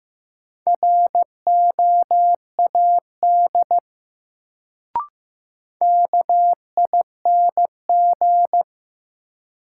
Morse code, heard as a keyed tone, about 15 wpm, 700 Hz. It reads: ROAD KING
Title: Sentences from Top 400 Words - New Word Drill - 15wpm